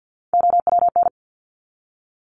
Morse code, heard as a keyed tone, about 52 wpm, 700 Hz. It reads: OWN
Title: Top 300 Words 50wpm